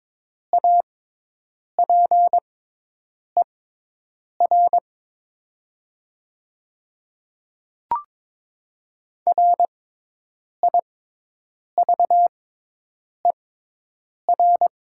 Morse code, heard as a keyed tone, about 22 wpm, spaced as for 8 wpm, 700 Hz. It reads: APER RIVER
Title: 6x Character Spacing - Top 300 Words - 22wpm